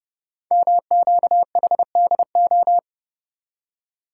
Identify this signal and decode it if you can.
Morse code, heard as a keyed tone, about 30 wpm, 700 Hz. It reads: MQHDO
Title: Code Groups - 5 Letters - 30wpm